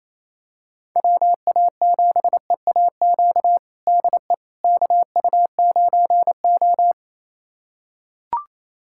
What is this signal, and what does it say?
Morse code, heard as a keyed tone, about 28 wpm, 700 Hz. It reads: WA7EAQ DE KU9O
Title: USA Callsign DE Callsign 28wpm